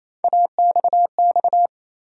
Morse code, heard as a keyed tone, about 28 wpm, 700 Hz. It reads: AXX
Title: USA Call Sign Postfixes - 3 Letters 28wpm